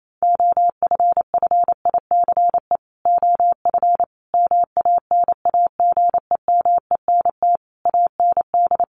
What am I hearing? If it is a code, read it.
Morse code, heard as a keyed tone, about 28 wpm, 700 Hz. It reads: OFFICE OF MANAGEMENT AND
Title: Sets of 5 Words Encore2 28wpm